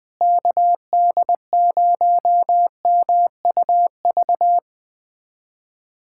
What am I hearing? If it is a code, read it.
Morse code, heard as a keyed tone, about 20 wpm, 700 Hz. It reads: KD0MUV
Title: USA Callsign DE Callsign 20wpm